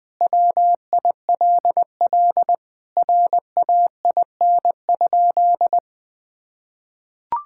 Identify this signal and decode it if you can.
Morse code, heard as a keyed tone, about 20 wpm, 700 Hz. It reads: WILL RAIN?
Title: Sentences with 4-Word Phrases - Spoken Beginning - 20wpm